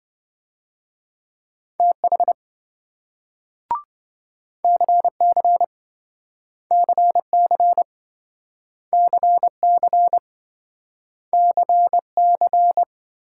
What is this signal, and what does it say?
Morse code, heard as a keyed tone, about 26 wpm, 700 Hz. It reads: TH CC CC CC CC